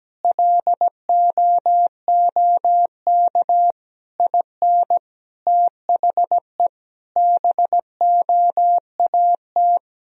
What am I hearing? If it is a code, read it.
Morse code, heard as a keyed tone, about 17 wpm, 700 Hz. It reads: LOOK IN THE BOAT
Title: Sentences from Top 1000 Words - Review Part 3 - 17wpm